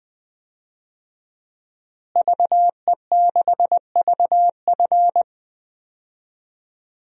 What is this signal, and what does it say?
Morse code, heard as a keyed tone, about 21 wpm, 700 Hz. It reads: VE6VF
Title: DXCC-100 Call Signs - 5 Characters - Repeated 2x - 20wpm